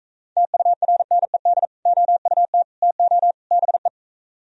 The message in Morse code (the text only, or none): TURNED OUT TO BE